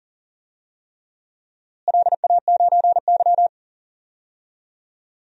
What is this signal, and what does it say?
Morse code, heard as a keyed tone, about 40 wpm, 700 Hz. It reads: LA9Y
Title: DXCC-100 Call Signs - 4 Characters - Repeated 3x - 40wpm